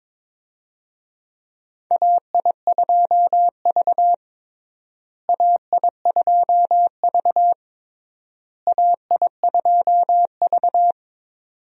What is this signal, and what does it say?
Morse code, heard as a keyed tone, about 22 wpm, 700 Hz. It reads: AI2V AI2V AI2V
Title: USA Call Signs - 4 Characters - Repeated 3x - 22wpm